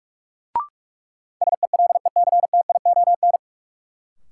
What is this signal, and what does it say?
Morse code, heard as a keyed tone, about 46 wpm, 700 Hz. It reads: SELECTION
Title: Sentences from Top 1400 Words - New Word Drill - 45wpm